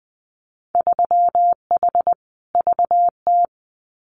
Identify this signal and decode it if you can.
Morse code, heard as a keyed tone, about 20 wpm, 700 Hz. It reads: 3H VT
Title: Winter Field Day Exchanges Encore 20wpm